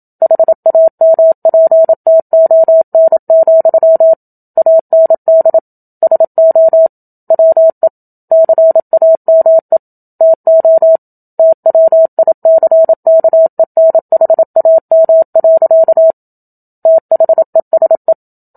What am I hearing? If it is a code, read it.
Morse code, heard as a keyed tone, about 27 wpm, 650 Hz. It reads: HAMPTON, AND SO WE CAME TO TWICKENHAM. THESE
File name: Book Two / 01-Chapter 1 - 27 WPM